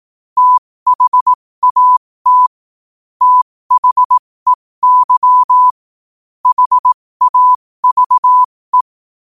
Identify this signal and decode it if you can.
Morse code, heard as a keyed tone, about 18 wpm, spaced as for 15 wpm, 1000 Hz. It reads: THAT THEY HAVE
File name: qotdfiles / 15wpm